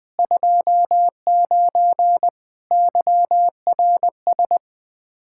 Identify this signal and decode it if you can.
Morse code, heard as a keyed tone, about 20 wpm, 700 Hz. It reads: 29 YRS
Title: Practice QSO - Exchange 3 - More Info - 20wpm